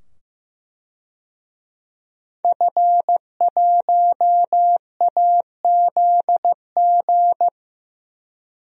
Morse code, heard as a keyed tone, about 15 wpm, 700 Hz. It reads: F1AZG
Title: DXCC-100 Call Signs - 5 Characters - 15wpm